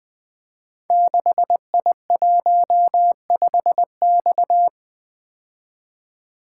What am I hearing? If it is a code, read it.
Morse code, heard as a keyed tone, about 20 wpm, 700 Hz. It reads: 6I15X
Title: Code Groups - 5 Letter:Numbers - 20wpm